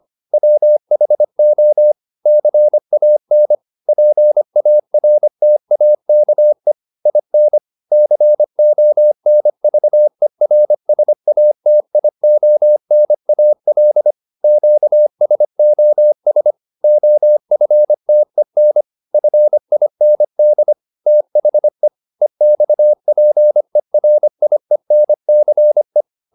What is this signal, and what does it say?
Morse code, heard as a keyed tone, about 25 wpm, 600 Hz. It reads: WHO CAN PARTAKE IN CONVERSATIONAL QSOS OFTEN FIND THE EXPERIENCE